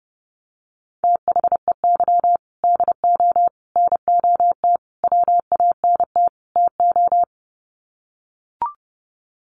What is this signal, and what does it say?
Morse code, heard as a keyed tone, about 30 wpm, 700 Hz. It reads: THEY DO NOT WANT TO
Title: Sets of 5 Words 30wpm